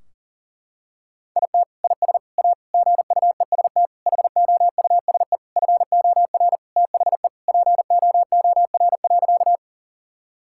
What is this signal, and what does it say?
Morse code, heard as a keyed tone, about 40 wpm, 700 Hz. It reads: IT IS A GUEST HOUSE FOR THE POOR.
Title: Sentences from Top 1600 Words 40wpm